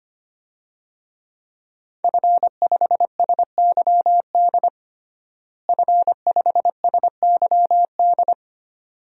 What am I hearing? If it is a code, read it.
Morse code, heard as a keyed tone, about 25 wpm, 700 Hz. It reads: F5SYD F5SYD
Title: DXCC-100 Call Signs - 5 Characters - Repeated 2x - 25wpm